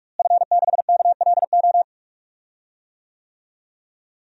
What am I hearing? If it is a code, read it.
Morse code, heard as a keyed tone, about 43 wpm, 700 Hz. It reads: F6KLO
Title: DXCC-100 Call Signs - 5 Characters - Repeated 3x - 45wpm